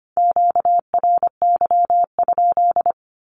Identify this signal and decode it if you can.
Morse code, heard as a keyed tone, about 25 wpm, 700 Hz. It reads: QRY?
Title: Q-Codes with Spoken Meaning - 25wpm